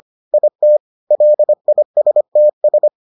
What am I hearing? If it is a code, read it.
Morse code, heard as a keyed tone, about 25 wpm, 600 Hz. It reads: IT LISTS